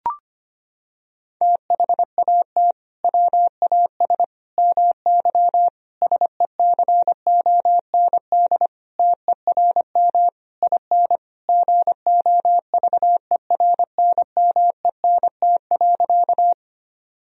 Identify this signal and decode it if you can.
Morse code, heard as a keyed tone, about 25 wpm, 700 Hz. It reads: THAT WAS MY SECOND TERM IN GOVERNMENT.
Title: Sentences from Top 1000 Words - Review Part 4 - 25wpm